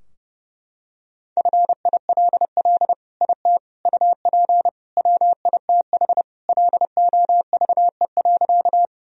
Morse code, heard as a keyed tone, about 32 wpm, 700 Hz. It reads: FILL IT UP WITH LOVE.
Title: Sentences from Top 1000 Words - Review Part 3 - 30wpm